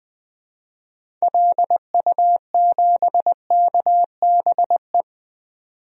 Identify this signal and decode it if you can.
Morse code, heard as a keyed tone, about 20 wpm, 700 Hz. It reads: LU7KBE
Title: DXCC-100 Call Signs - 6 Characters - 20wpm